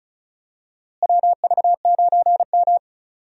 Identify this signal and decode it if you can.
Morse code, heard as a keyed tone, about 36 wpm, 700 Hz. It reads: WV9M
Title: USA Call Signs - 4 Characters - Repeated 2x - 35wpm